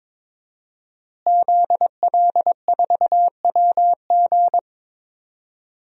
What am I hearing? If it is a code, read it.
Morse code, heard as a keyed tone, about 22 wpm, 700 Hz. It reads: ZL4WG
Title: DXCC-100 Call Signs - 5 Characters - Repeated 3x - 22wpm